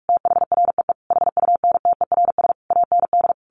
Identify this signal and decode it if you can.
Morse code, heard as a keyed tone, about 44 wpm, 700 Hz. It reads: THREE HUNTERS AND
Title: A Sound Of Thunder 45wpm